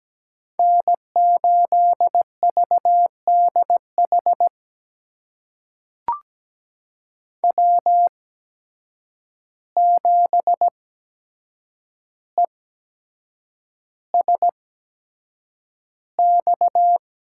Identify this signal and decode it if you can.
Morse code, heard as a keyed tone, about 17 wpm, spaced as for 7 wpm, 700 Hz. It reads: N8VDH W7ESX
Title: USA Call Signs - 5 Characters - 8x Character Spacing - 17wpm